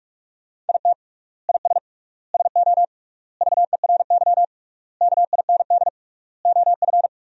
Morse code, heard as a keyed tone, about 42 wpm, 700 Hz. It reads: IT IS SO VERY KIND OF
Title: 3x Word-Spacing - Sentences from Top 300 Words - 45wpm